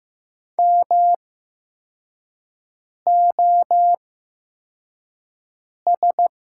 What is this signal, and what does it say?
Morse code, heard as a keyed tone, about 15 wpm, spaced as for 4 wpm, 700 Hz. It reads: MOS
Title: 8x Character Spacing - Top 100 Words - 15wpm